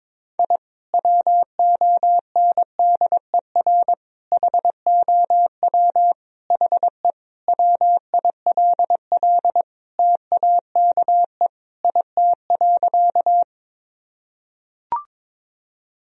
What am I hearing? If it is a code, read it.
Morse code, heard as a keyed tone, about 22 wpm, 700 Hz. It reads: I WONDER HOW HE WILL TAKE IT.
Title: Sentences from Top 1000 Words - Review Part 3 - 22wpm